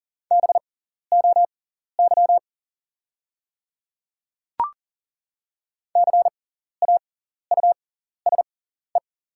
Morse code, heard as a keyed tone, about 40 wpm, spaced as for 14 wpm, 690 Hz. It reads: BOY CAUSE
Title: 6x Character Spacing - Top 300 Words - 40wpm